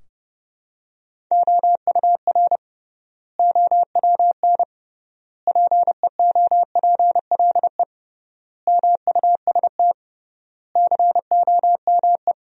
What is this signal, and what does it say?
Morse code, heard as a keyed tone, about 30 wpm, 700 Hz. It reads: OUR OWN PEOPLE MUST COME
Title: 3x Word-Spacing - Sentences from Top 300 Words - 30wpm